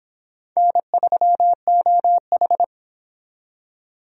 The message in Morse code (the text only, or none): N3OH